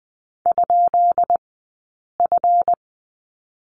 Morse code, heard as a keyed tone, about 20 wpm, spaced as for 8 wpm, 700 Hz. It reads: ?F
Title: 183 Lesson 30d - Letter b - Two Characters 20wpm